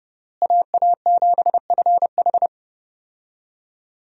Morse code, heard as a keyed tone, about 32 wpm, 700 Hz. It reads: AA7FH
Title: USA Call Signs - 5 Characters - Repeated 3x - 30wpm